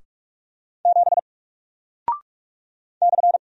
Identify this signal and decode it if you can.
Morse code, heard as a keyed tone, about 44 wpm, 700 Hz. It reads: 7 /
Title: Single Character (Letter:Number:Punctuation:Prosign) Alt 45wpm